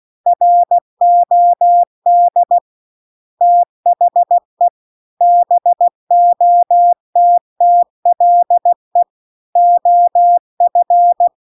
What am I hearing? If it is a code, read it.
Morse code, heard as a keyed tone, about 16 wpm, 700 Hz. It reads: ROD THE BOTTLE OF